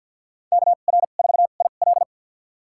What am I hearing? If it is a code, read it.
Morse code, heard as a keyed tone, about 53 wpm, spaced as for 37 wpm, 700 Hz. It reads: KR4IL
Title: USA Call Signs - 5 Characters - 2x Character Spacing - 50wpm